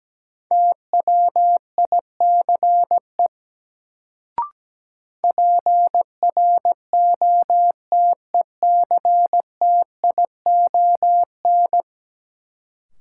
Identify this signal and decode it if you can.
Morse code, heard as a keyed tone, about 17 wpm, 700 Hz. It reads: TWICE PROTECTION
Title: Sentences from Top 1600 Words - New Word Drill - 17wpm